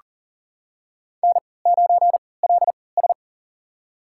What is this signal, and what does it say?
Morse code, heard as a keyed tone, about 39 wpm, spaced as for 23 wpm, 700 Hz. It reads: N9LS
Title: USA Call Signs - 4 Characters - 3x Character Spacing - 40wpm